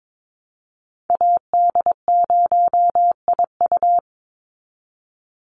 Morse code, heard as a keyed tone, about 22 wpm, 700 Hz. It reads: AD0IU
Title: Call Signs - 22wpm